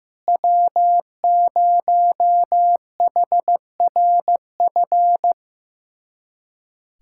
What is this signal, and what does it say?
Morse code, heard as a keyed tone, about 15 wpm, 700 Hz. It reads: W0HRF
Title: USA Callsign DE Callsign - 5 Characters - 15wpm